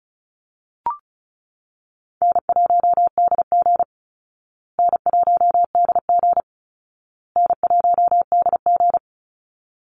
Morse code, heard as a keyed tone, about 34 wpm, 700 Hz. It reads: N1DG N1DG N1DG
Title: Easy Call Signs Repeated 3x - 35wpm